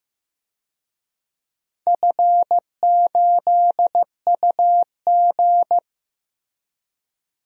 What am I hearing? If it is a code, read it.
Morse code, heard as a keyed tone, about 16 wpm, 700 Hz. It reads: F8UG
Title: DXCC-100 Call Signs - 4 Characters - Repeated 2x - 15wpm